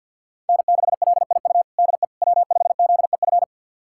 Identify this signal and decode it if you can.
Morse code, heard as a keyed tone, about 52 wpm, 700 Hz. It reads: N6LIU DE W5ZEF
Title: USA Callsign DE Callsign - 5 Characters - 50wpm